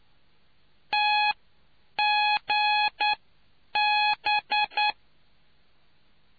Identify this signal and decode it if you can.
Morse code, heard as a keyed tone, about 9 wpm, spaced as for 7 wpm, 840 Hz. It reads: TGB